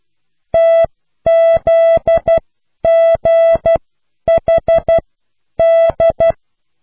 Morse code, heard as a keyed tone, about 12 wpm, spaced as for 10 wpm, 660 Hz. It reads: TZGHD